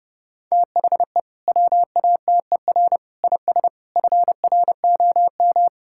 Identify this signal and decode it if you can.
Morse code, heard as a keyed tone, about 30 wpm, 700 Hz. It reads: THE WATER IS FROM
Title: Sentences from Top 1000 Words - Review Part 5 - 30wpm